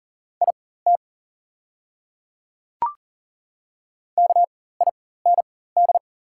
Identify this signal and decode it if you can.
Morse code, heard as a keyed tone, about 39 wpm, spaced as for 18 wpm, 700 Hz. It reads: IT KIND
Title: 4x Character Spacing - Top 200 Words - 40wpm